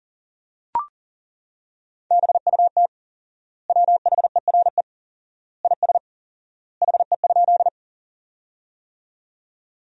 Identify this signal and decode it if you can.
Morse code, heard as a keyed tone, about 40 wpm, 700 Hz. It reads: BUT WHERE IS HE?
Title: 4x Word-Spacing - Sentences from Top 200 Words - 40wpm